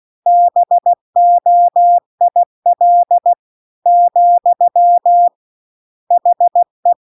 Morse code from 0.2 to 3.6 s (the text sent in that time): BOIL